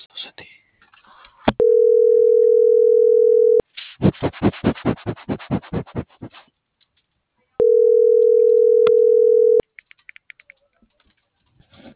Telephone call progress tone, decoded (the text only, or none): ringback tone